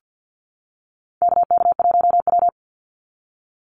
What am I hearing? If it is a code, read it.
Morse code, heard as a keyed tone, about 50 wpm, 700 Hz. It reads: KK1W
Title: NA SOTA Activator Callsigns 50wpm